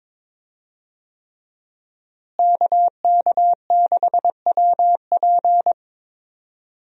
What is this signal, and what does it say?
Morse code, heard as a keyed tone, about 22 wpm, 700 Hz. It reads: KK6WP
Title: USA Call Signs - 5 Characters - Repeated 3x - 22wpm